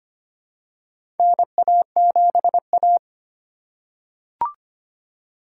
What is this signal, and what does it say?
Morse code, heard as a keyed tone, about 25 wpm, 700 Hz. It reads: NA7A
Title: USA Call Signs - 4 Characters - 4x Character Spacing - 25wpm